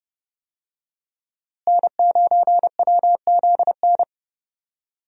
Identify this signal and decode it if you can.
Morse code, heard as a keyed tone, about 32 wpm, 700 Hz. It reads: N9WZN
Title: USA Call Signs - 5 Characters - Repeated 3x - 30wpm